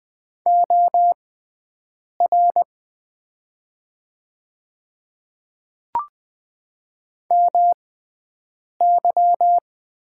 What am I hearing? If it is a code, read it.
Morse code, heard as a keyed tone, about 20 wpm, spaced as for 7 wpm, 700 Hz. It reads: OR MY
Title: 6x Character Spacing - Top 500 Words - 20wpm